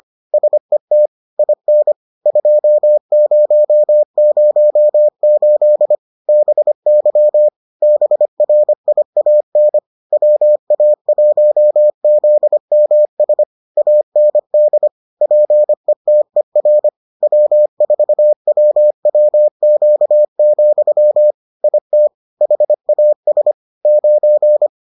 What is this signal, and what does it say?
Morse code, heard as a keyed tone, about 25 wpm, 600 Hz. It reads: SET IN 2008 BY BRIAN WA1ZMS AND PETER W4WWQ, IT HAS 9